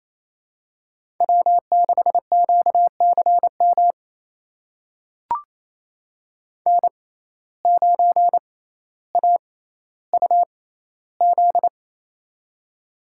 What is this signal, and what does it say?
Morse code, heard as a keyed tone, about 28 wpm, spaced as for 14 wpm, 700 Hz. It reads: W6QCM N9AUZ
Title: USA Call Signs - 5 Characters - 6x Character Spacing - 28wpm